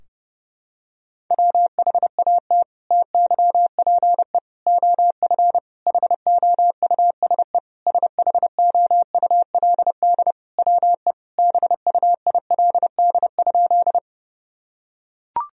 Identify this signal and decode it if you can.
Morse code, heard as a keyed tone, about 32 wpm, 700 Hz. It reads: WHAT TYPE OF HOUSE SHOULD WE BUILD?
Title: Sentences from Top 1000 Words - Review Part 3 - 30wpm